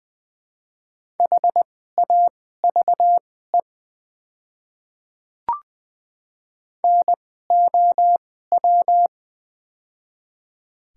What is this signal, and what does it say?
Morse code, heard as a keyed tone, about 20 wpm, spaced as for 14 wpm, 700 Hz. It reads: HAVE NOW
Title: 2x Character Spacing - Top 100 Words - 20wpm